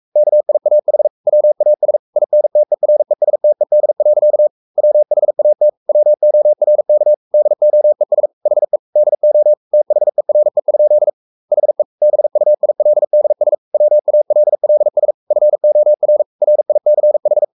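Morse code, heard as a keyed tone, about 44 wpm, 600 Hz. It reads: KIAH WAS INTERESTED. WHAT WORK DOES HE DO THERE? HE BUILDS WALLS FOR RICH
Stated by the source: Morse code at 43 wpm sending Ken Follett's novel Never in English, from CW med Gnister.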